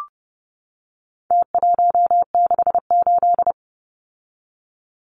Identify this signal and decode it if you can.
Morse code, heard as a keyed tone, about 30 wpm, 700 Hz. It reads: T168
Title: Numbers - Contest Style - 4-Digit 30wpm